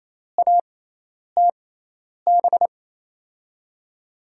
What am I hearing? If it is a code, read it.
Morse code, heard as a keyed tone, about 28 wpm, spaced as for 10 wpm, 700 Hz. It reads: ATB
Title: USA Call Signs - 6x Character Spacing - 28wpm